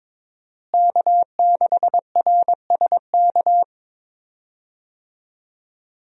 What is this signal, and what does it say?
Morse code, heard as a keyed tone, about 22 wpm, 700 Hz. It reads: K6RSK
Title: USA Call Signs - 5 Characters - Repeated 2x - 22wpm